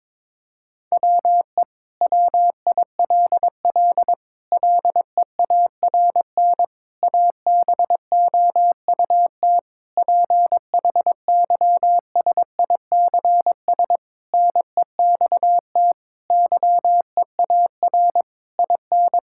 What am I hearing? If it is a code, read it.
Morse code, heard as a keyed tone, about 22 wpm, 700 Hz. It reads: WE WILL LEARN ABOUT PHYSICS NEXT YEAR IN
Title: Sentences from Top 1000 Words - Review Part 5 - 22wpm